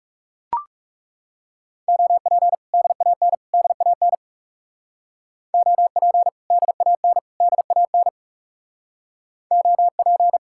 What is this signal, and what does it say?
Morse code, heard as a keyed tone, about 42 wpm, 700 Hz. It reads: OP DAN DAN OP DAN DAN OP